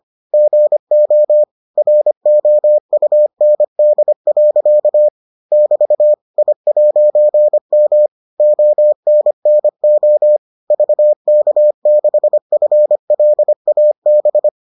Morse code, heard as a keyed tone, about 25 wpm, 600 Hz. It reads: GO ROUND. <BT> I'M ONNO VK6FLAB